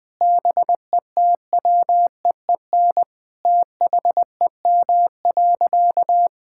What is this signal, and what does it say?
Morse code, heard as a keyed tone, about 20 wpm, 700 Hz. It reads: BETWEEN THEM.
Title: Sentences from Top 1000 Words - Review Part 6 - 20wpm